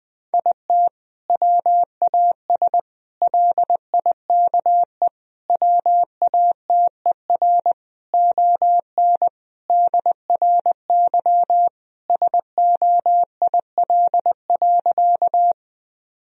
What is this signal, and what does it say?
Morse code, heard as a keyed tone, about 21 wpm, 700 Hz. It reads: IT WAS LIKE WATER ON DRY SOIL.
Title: Sentences from Top 1000 Words - Review Part 4 - 20wpm